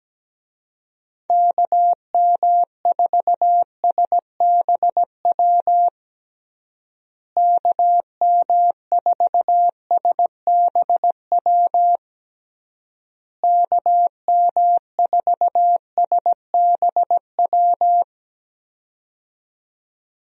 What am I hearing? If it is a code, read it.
Morse code, heard as a keyed tone, about 17 wpm, 700 Hz. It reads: KM4SBW KM4SBW KM4SBW
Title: USA Call Signs - 6 Characters - Repeated 3x - 17wpm